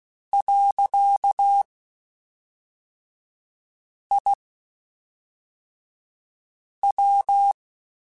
Morse code, heard as a keyed tone, about 16 wpm, 800 Hz. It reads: . I W